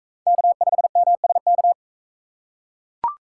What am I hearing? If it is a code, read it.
Morse code, heard as a keyed tone, about 42 wpm, 700 Hz. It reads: K5MSK